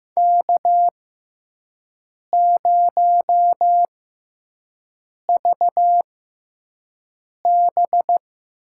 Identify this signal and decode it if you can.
Morse code, heard as a keyed tone, about 15 wpm, spaced as for 5 wpm, 700 Hz. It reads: K0VB
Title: USA Call Signs - 4 Characters - 6x Character Spacing - 15wpm